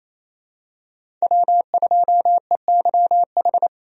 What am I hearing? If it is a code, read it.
Morse code, heard as a keyed tone, about 28 wpm, 700 Hz. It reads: W2EYH
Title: USA Call Signs - 5 Characters - Repeated 2x - 28wpm